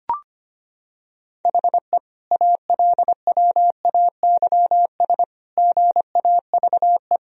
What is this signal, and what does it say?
Morse code, heard as a keyed tone, about 25 wpm, 700 Hz. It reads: HE ALWAYS GAVE
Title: Sentences from Top 1000 Words - Review Part 3 - 25wpm